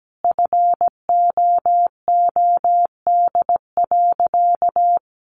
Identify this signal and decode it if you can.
Morse code, heard as a keyed tone, about 17 wpm, 700 Hz. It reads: FOOD.
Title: Sentences from Top 1200 Words 17wpm